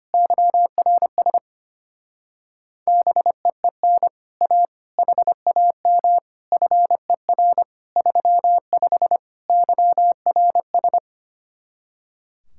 Morse code, heard as a keyed tone, about 25 wpm, 700 Hz. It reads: YRS BEEN A HAM FER 35 YRS
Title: Speed-Racing Common QSO Phrases Encore - 25wpm